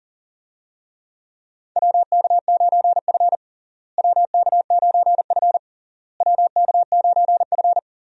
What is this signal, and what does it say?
Morse code, heard as a keyed tone, about 37 wpm, 700 Hz. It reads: WK9F WK9F WK9F
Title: USA Call Signs - 4 Characters - Repeated 3x - 40wpm